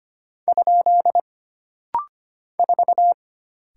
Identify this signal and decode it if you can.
Morse code, heard as a keyed tone, about 25 wpm, 700 Hz. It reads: ? 4
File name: Single Character (Letter:Number:Punctuation:Prosign) - Rapid-Fire - 25wpm